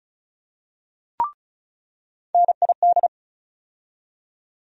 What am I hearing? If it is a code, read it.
Morse code, heard as a keyed tone, about 36 wpm, 700 Hz. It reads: NID
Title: USA Call Sign Postfixes - 3 Letters 35wpm